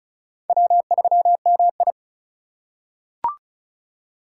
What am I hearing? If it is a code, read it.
Morse code, heard as a keyed tone, about 36 wpm, 700 Hz. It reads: W3MI
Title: USA Call Signs - 4 Characters - 3x Character Spacing - 35wpm